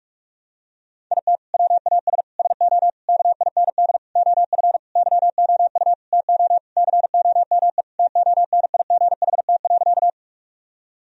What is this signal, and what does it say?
Morse code, heard as a keyed tone, about 46 wpm, 700 Hz. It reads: IT WAS SO KIND OF YOU TO COME TONIGHT.
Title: Sentences from Top 1000 Words - Review Part 3 - 45wpm